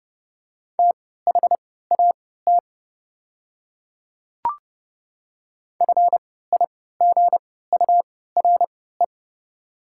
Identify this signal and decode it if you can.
Morse code, heard as a keyed tone, about 30 wpm, spaced as for 17 wpm, 700 Hz. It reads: THAT FIGURE
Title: 3x Character Spacing - Top 500 Words - 30wpm